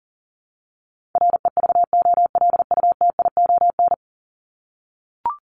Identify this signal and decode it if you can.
Morse code, heard as a keyed tone, about 40 wpm, 700 Hz. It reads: REVOLUTION
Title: Top 1751-2000 Words 40wpm